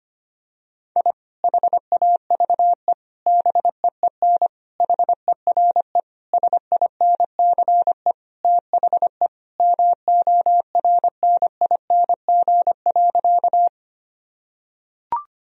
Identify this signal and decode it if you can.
Morse code, heard as a keyed tone, about 25 wpm, 700 Hz. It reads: I HAVE BEEN HERE SINCE THE MORNING.
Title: Sentences from Top 1000 Words - Review Part 1 - 25wpm